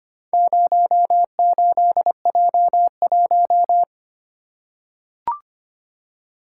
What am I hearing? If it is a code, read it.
Morse code, heard as a keyed tone, about 25 wpm, 700 Hz. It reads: 08J1
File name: practice / 25wpm / Four Letter-Number Combinations - 25wpm